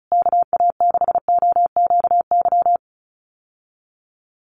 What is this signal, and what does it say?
Morse code, heard as a keyed tone, about 34 wpm, 700 Hz. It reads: KA6OQY
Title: USA Call Signs Repeated 2x 35wpm